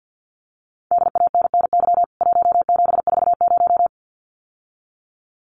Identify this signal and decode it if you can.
Morse code, heard as a keyed tone, about 49 wpm, 700 Hz. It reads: DANNY 1740
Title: CWT Exchange 50wpm